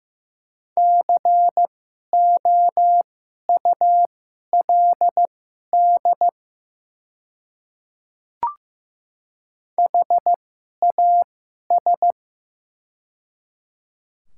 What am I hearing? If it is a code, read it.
Morse code, heard as a keyed tone, about 15 wpm, spaced as for 11 wpm, 700 Hz. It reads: COULD HAS
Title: 2x Character Spacing - Top 200 Words - 15wpm